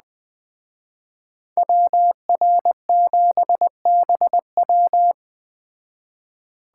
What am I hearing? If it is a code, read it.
Morse code, heard as a keyed tone, about 20 wpm, 700 Hz. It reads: WR7BW